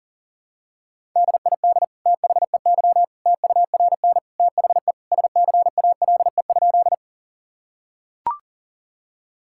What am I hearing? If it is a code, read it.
Morse code, heard as a keyed tone, about 37 wpm, 700 Hz. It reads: DID THEY TURN THE SCALE?
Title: Sentences from Top 1000 Words - Review Part 5 - 40wpm